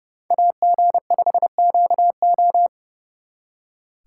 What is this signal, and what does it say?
Morse code, heard as a keyed tone, about 38 wpm, 700 Hz. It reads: AG5QO